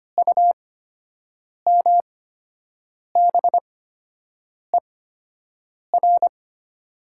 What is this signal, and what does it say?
Morse code, heard as a keyed tone, about 25 wpm, spaced as for 7 wpm, 700 Hz. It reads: UMBER
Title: 8x Character Spacing - Top 200 Words - 25wpm